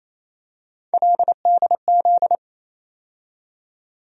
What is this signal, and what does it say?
Morse code, heard as a keyed tone, about 28 wpm, 700 Hz. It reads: LDZ
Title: USA Call Sign Postfixes - 3 Letters 28wpm